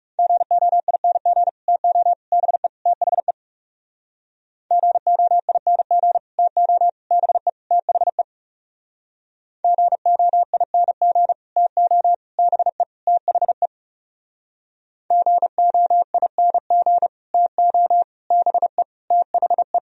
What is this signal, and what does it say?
Morse code, heard as a keyed tone, about 37 wpm, 700 Hz. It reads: GOING TO BE THE GOING TO BE THE GOING TO BE THE GOING TO BE THE
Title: Speed-Racing - Sets of 4 Words 30wpm